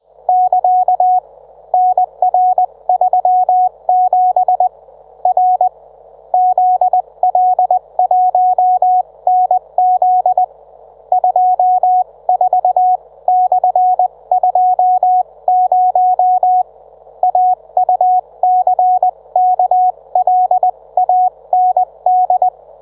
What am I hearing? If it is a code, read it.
Morse code, heard as a keyed tone, about 20 wpm, spaced as for 18 wpm, 730 Hz. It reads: <KA> NR37 R ZL1NZ 24/20 AUCKLAND